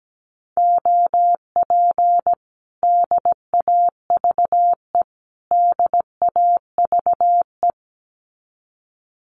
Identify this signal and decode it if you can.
Morse code, heard as a keyed tone, about 17 wpm, 700 Hz. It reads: OP DAVE DAVE
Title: Common QSO Phrases Encore 17wpm